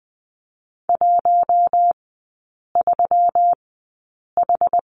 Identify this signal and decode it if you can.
Morse code, heard as a keyed tone, about 20 wpm, spaced as for 8 wpm, 700 Hz. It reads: 13H
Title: 138 Lesson 23e - Letter f - Three Characters 20wpm